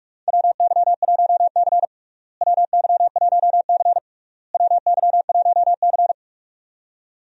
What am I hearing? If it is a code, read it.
Morse code, heard as a keyed tone, about 45 wpm, 700 Hz. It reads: WY1C WY1C WY1C
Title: USA Call Signs - 4 Characters - Repeated 3x - 45wpm